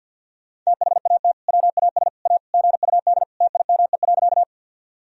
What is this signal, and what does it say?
Morse code, heard as a keyed tone, about 47 wpm, 700 Hz. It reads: THAT WAS A GUD TIME.
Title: Sentences from Top 200 Words and Abbreviations 50wpm